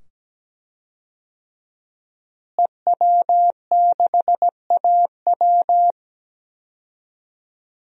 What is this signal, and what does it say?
Morse code, heard as a keyed tone, about 17 wpm, 700 Hz. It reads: EW6AW
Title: DXCC-100 Call Signs - 5 Characters - Repeated 2x - 17wpm